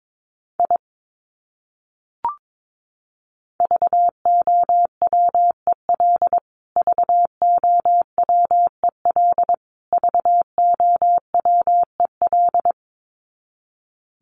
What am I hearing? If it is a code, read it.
Morse code, heard as a keyed tone, about 22 wpm, 700 Hz. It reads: I VOWEL VOWEL VOWEL
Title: Top 500 Words Repeated 22wpm